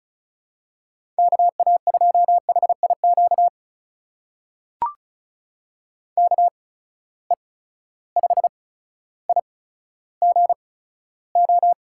Morse code, heard as a keyed tone, about 36 wpm, spaced as for 15 wpm, 700 Hz. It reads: KA2HIQ KE5IGO
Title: USA Call Signs - 6 Characters - 8x Character Spacing - 35wpm